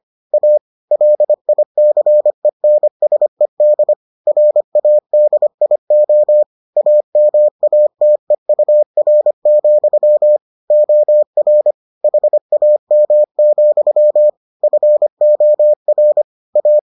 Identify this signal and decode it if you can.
Morse code, heard as a keyed tone, about 25 wpm, 600 Hz. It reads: A LICENSED RADIO AMATEUR, OR HAM, FOR A